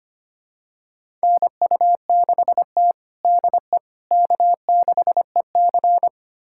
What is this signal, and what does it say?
Morse code, heard as a keyed tone, about 25 wpm, 700 Hz. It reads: NU6T DE K6EC
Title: USA Callsign DE Callsign - 4 Characters - 25wpm